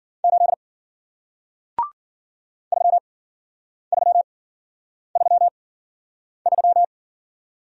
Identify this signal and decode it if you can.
Morse code, heard as a keyed tone, about 53 wpm, spaced as for 9 wpm, 700 Hz. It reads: 8 3333